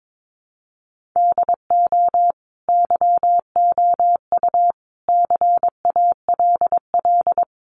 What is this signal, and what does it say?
Morse code, heard as a keyed tone, about 22 wpm, 700 Hz. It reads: DO YOU CALL
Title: Sentences from Top 1500 Words 22wpm